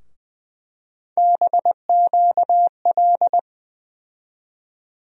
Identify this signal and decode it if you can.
Morse code, heard as a keyed tone, about 20 wpm, 700 Hz. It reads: BQL
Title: USA Call Sign Postfixes - 3 Letters 20wpm